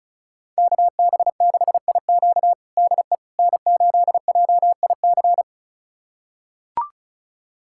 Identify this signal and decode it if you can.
Morse code, heard as a keyed tone, about 36 wpm, 700 Hz. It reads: KB6IQ DE N8JIC
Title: USA Callsign DE Callsign - 5 Characters - 35wpm